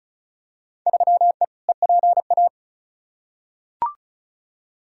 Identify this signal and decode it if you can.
Morse code, heard as a keyed tone, about 36 wpm, 700 Hz. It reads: 3E EPA
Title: Field Day Exchanges Encore - 35wpm